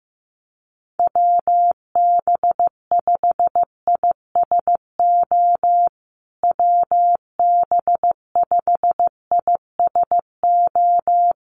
Morse code, heard as a keyed tone, about 15 wpm, 700 Hz. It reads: WB5ISO WB5ISO
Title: USA Call Signs Repeated 3x 15wpm